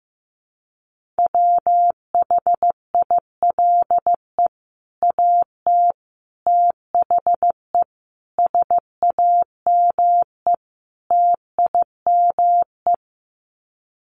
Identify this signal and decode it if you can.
Morse code, heard as a keyed tone, about 15 wpm, 700 Hz. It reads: WHILE AT THE SAME TIME
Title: Sets of 5 Words 15wpm